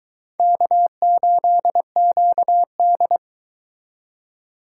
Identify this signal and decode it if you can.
Morse code, heard as a keyed tone, about 22 wpm, 700 Hz. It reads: K8QD